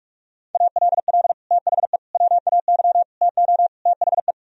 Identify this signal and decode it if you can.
Morse code, heard as a keyed tone, about 47 wpm, 700 Hz. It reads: ALL THE WAY TO THE